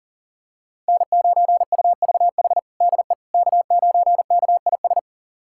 Morse code, heard as a keyed tone, about 40 wpm, 700 Hz. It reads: N9UVH DE K9KIS
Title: USA Callsign DE Callsign - 5 Characters - 40wpm